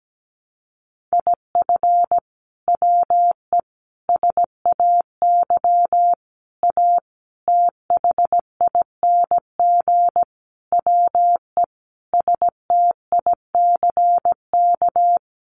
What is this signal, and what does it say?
Morse code, heard as a keyed tone, about 17 wpm, 700 Hz. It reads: IF WE SAY A THING WE STICK
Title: Sentences from Top 1200 Words 17wpm